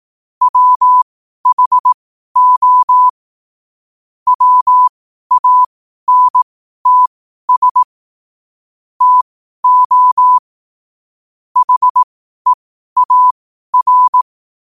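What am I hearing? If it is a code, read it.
Morse code, heard as a keyed tone, about 18 wpm, spaced as for 12 wpm, 1000 Hz. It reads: WHO WANTS TO HEAR
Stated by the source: Morse code at 12 wpm.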